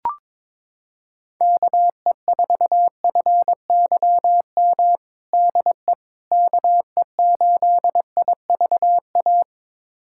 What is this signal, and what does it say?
Morse code, heard as a keyed tone, about 22 wpm, 700 Hz. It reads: KE4FYM DE KE8IVA
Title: USA Callsign DE Callsign - 6 Characters - 22wpm